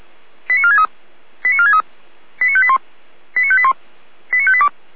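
Five-tone selective call (ZVEI), heard as quick runs of five tones, tones of about 80 ms. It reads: 89463 89463 89581 89581 89582
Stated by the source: Five-tone ZVEI selective call received on SDR.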